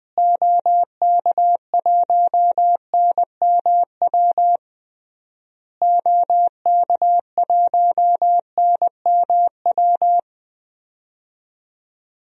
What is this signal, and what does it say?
Morse code, heard as a keyed tone, about 20 wpm, 700 Hz. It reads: OK1NMW OK1NMW
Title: DXCC-100 Call Signs - 6 Characters - Repeated 3x - 20wpm